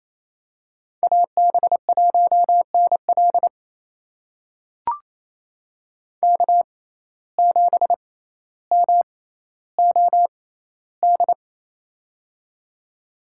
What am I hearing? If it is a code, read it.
Morse code, heard as a keyed tone, about 28 wpm, spaced as for 14 wpm, 700 Hz. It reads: AB1NL K7MOD
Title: USA Call Signs - 5 Characters - 6x Character Spacing - 28wpm